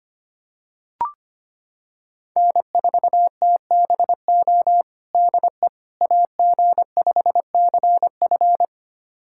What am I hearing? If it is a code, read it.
Morse code, heard as a keyed tone, about 25 wpm, 700 Hz. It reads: N4TBO DE AG5CF
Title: USA Callsign DE Callsign - 5 Characters - 25wpm